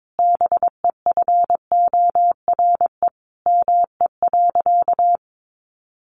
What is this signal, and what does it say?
Morse code, heard as a keyed tone, about 22 wpm, 700 Hz. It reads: BEFORE ME.
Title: Sentences from Top 1300 Words 22wpm